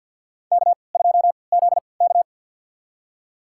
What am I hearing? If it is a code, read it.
Morse code, heard as a keyed tone, about 53 wpm, spaced as for 29 wpm, 700 Hz. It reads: K2ZK
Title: USA Call Signs - 4 Characters - 3x Character Spacing - 50wpm